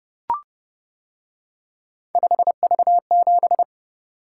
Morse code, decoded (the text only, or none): 5V7